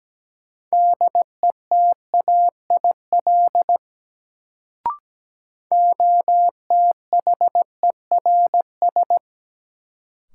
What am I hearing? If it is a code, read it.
Morse code, heard as a keyed tone, about 17 wpm, 700 Hz. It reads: DETAIL OTHERS
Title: Sentences from Top 1500 Words - New Word Drill Alt - 17wpm